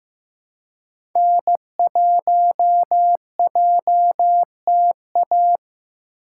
Morse code, heard as a keyed tone, about 15 wpm, 700 Hz. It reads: N1JTA
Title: USA Call Signs - 5 Characters - 3x Character Spacing - 15wpm